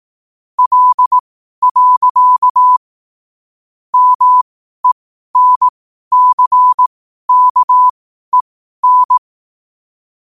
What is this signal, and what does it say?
Morse code, heard as a keyed tone, about 18 wpm, spaced as for 12 wpm, 1000 Hz. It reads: L. MENCKEN
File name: qotdfiles / 12wpm